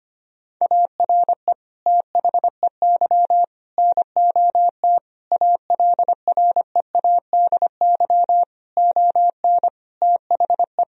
Morse code, heard as a keyed tone, about 25 wpm, 700 Hz. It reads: ARE THEY NOT ALREADY ON THE
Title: Sentences from Top 1000 Words - Review Part 1 - 25wpm